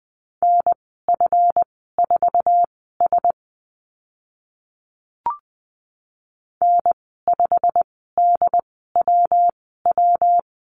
Morse code, heard as a keyed tone, about 20 wpm, spaced as for 14 wpm, 700 Hz. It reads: NF4S N5DWW
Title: 140 Lesson 23g - Letter f - Call Signs 20wpm